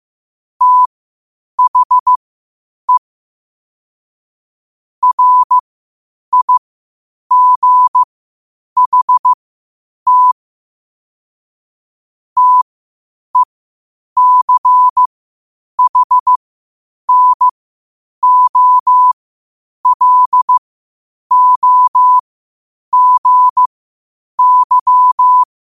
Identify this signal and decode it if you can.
Morse code, heard as a keyed tone, about 15 wpm, spaced as for 8 wpm, 1000 Hz. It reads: THE RIGHT TECHNOLOGY
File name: 8wpm